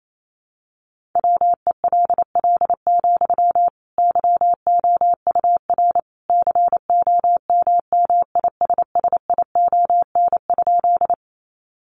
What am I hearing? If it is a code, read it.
Morse code, heard as a keyed tone, about 28 wpm, 700 Hz. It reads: WELL, YOUR COMMISSION?
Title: Sentences from Top 1400 Words 28wpm